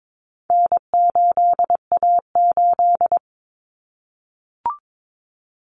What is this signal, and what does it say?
Morse code, heard as a keyed tone, about 22 wpm, 700 Hz. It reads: N8A8
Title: Common Cut Numbers - 4-Digit 22wpm